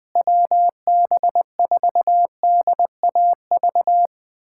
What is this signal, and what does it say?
Morse code, heard as a keyed tone, about 21 wpm, 700 Hz. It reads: WB4DAV
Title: USA Callsign DE Callsign 20wpm